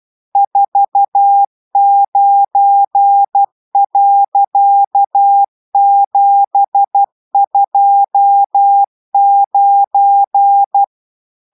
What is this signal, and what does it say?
Morse code, heard as a keyed tone, about 12 wpm, 800 Hz. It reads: 49.729